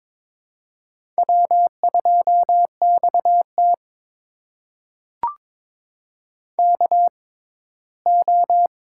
Morse code, heard as a keyed tone, about 22 wpm, spaced as for 15 wpm, 700 Hz. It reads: W2XT KO
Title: USA Call Signs - 4 Characters - 6x Character Spacing - 22wpm